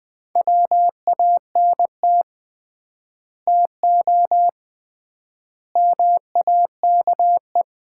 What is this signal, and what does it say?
Morse code, heard as a keyed tone, about 20 wpm, 700 Hz. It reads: WANT TO MAKE
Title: 3x Word-Spacing - Sets of 5 Words - 20wpm